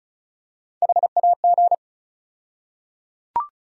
Morse code, heard as a keyed tone, about 36 wpm, 700 Hz. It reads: HAG
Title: USA Call Sign Postfixes - 3 Letters 35wpm